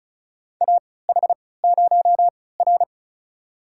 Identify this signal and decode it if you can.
Morse code, heard as a keyed tone, about 34 wpm, spaced as for 20 wpm, 700 Hz. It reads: AH0R
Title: USA Call Signs - 4 Characters - 3x Character Spacing - 35wpm